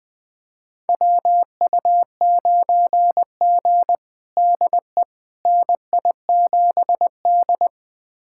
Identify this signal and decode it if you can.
Morse code, heard as a keyed tone, about 20 wpm, 700 Hz. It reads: WU9G DE NI7D
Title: USA Callsign DE Callsign - 4 Characters - 20wpm